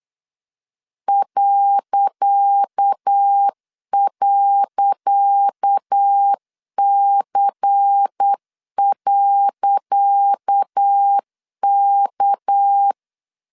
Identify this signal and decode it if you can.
Morse code, heard as a keyed tone, about 8 wpm, 800 Hz. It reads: ..C.K
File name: LEZIONI / 10 SIGNS / 10C DOT 8wpm 3649